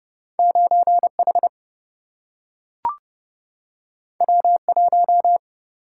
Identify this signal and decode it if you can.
Morse code, heard as a keyed tone, about 30 wpm, 700 Hz. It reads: 9 H W 1
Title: Single-Letter Single-Number 30wpm